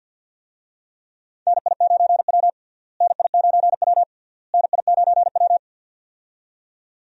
Morse code, heard as a keyed tone, about 49 wpm, 700 Hz. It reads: NI9W NI9W NI9W
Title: USA Call Signs - 4 Characters - Repeated 3x - 50wpm